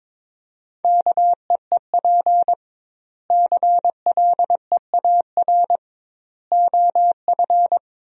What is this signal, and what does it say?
Morse code, heard as a keyed tone, about 23 wpm, 700 Hz. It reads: KEEP CLEAR OF
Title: 2x Word-Spacing - Sentences from Top 500 Words - 22wpm